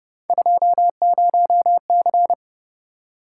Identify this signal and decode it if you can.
Morse code, heard as a keyed tone, about 30 wpm, 700 Hz. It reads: 20C